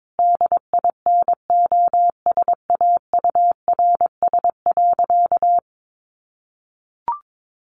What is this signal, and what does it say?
Morse code, heard as a keyed tone, about 22 wpm, 700 Hz. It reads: DINOSAURS.
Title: A Sound Of Thunder 22wpm